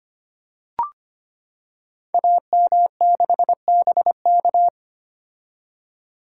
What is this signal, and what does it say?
Morse code, heard as a keyed tone, about 25 wpm, 700 Hz. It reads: AM6BK
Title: Code Groups - 5 Letter:Numbers - 25wpm